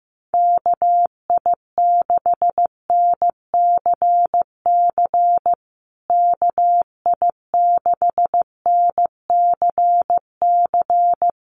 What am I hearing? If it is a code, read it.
Morse code, heard as a keyed tone, about 15 wpm, 700 Hz. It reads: KI6NCC KI6NCC
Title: USA Call Signs Repeated 3x 15wpm